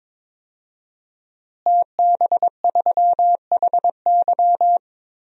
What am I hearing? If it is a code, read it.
Morse code, heard as a keyed tone, about 22 wpm, 700 Hz. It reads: TB3HY
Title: DXCC-100 Call Signs - 5 Characters - Repeated 3x - 22wpm